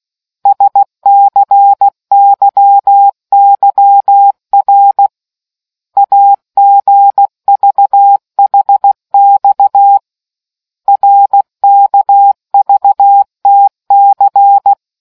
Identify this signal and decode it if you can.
Morse code, heard as a keyed tone, about 16 wpm, 800 Hz. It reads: SCYYR AGVHX RKVTC